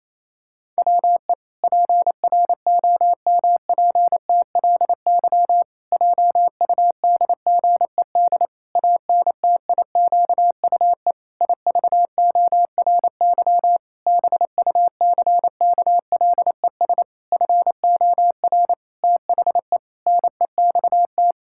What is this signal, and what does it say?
Morse code, heard as a keyed tone, about 28 wpm, 700 Hz. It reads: WE PROMPTLY JUDGED ANTIQUE IVORY BUCKLES FOR THE NEXT
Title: Pangram Challenge v2 - 28wpm